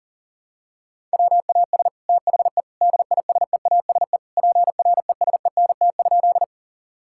Morse code, heard as a keyed tone, about 40 wpm, 700 Hz. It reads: WAS THE DISEASE PRESENT?
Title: Sentences from Top 1000 Words - Review Part 6 - 40wpm